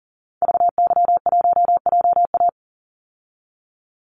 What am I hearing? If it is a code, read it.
Morse code, heard as a keyed tone, about 40 wpm, 700 Hz. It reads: VY1JA
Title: Call Signs Repeated 3x - 40wpm